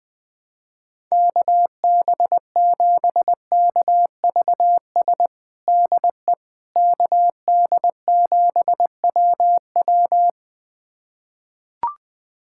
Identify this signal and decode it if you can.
Morse code, heard as a keyed tone, about 20 wpm, 700 Hz. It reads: KB7KVS DE KD7WW
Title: USA Callsign DE Callsign 20wpm